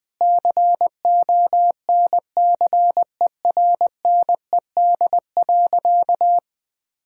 Morse code, heard as a keyed tone, about 20 wpm, 700 Hz. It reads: CONCERNED.
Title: Sentences from Top 1000 Words - Review Part 6 - 20wpm